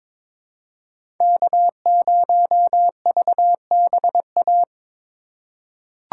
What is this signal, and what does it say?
Morse code, heard as a keyed tone, about 22 wpm, 700 Hz. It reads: K0VBA
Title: USA Call Signs - 5 Characters - 3x Character Spacing - 22wpm